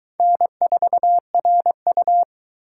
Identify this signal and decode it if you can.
Morse code, heard as a keyed tone, about 22 wpm, 700 Hz. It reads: N4RU